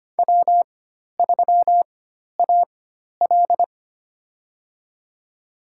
Morse code, heard as a keyed tone, about 25 wpm, spaced as for 12 wpm, 700 Hz. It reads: W3AL
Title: USA Call Signs - 4 Characters - 4x Character Spacing - 25wpm